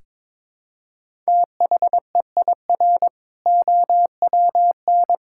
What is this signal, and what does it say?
Morse code, heard as a keyed tone, about 21 wpm, 700 Hz. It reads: THEIR OWN
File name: Build A Phrase - 2-Word Phrases - 22wpm